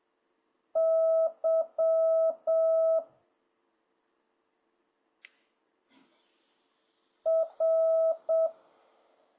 Morse code, heard as a keyed tone, about 7 wpm, 650 Hz. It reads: Y R